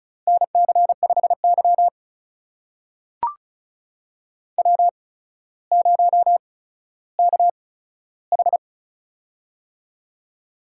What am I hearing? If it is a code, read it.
Morse code, heard as a keyed tone, about 36 wpm, spaced as for 15 wpm, 700 Hz. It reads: NC5Y W0KH
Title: USA Call Signs - 8x Character Spacing - 35wpm